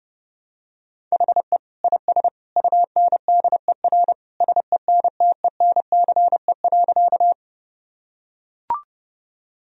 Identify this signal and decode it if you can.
Morse code, heard as a keyed tone, about 30 wpm, 700 Hz. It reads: HE IS UNDER SENTENCE.
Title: Sentences from Top 1000 Words - Review Part 1 - 30wpm